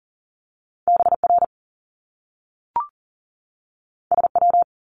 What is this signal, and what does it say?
Morse code, heard as a keyed tone, about 39 wpm, 700 Hz. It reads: BR SW
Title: Top Two Letter Combinations in Words 40wpm